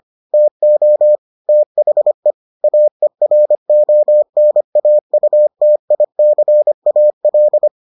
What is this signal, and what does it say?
Morse code, heard as a keyed tone, about 25 wpm, 600 Hz. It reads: TO THE AERONAUTICAL